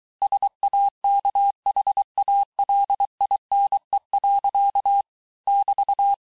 Morse code, heard as a keyed tone, about 23 wpm, 800 Hz. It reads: SAKHALINE. <BT>
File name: texte-clair-vitesse-17mpm-puis-22mpm